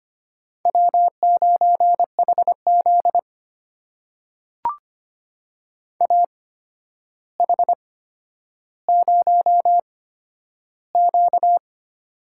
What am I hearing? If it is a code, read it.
Morse code, heard as a keyed tone, about 25 wpm, spaced as for 11 wpm, 700 Hz. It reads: W9HZ AH0Q
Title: USA Call Signs - 4 Characters - 8x Character Spacing - 25wpm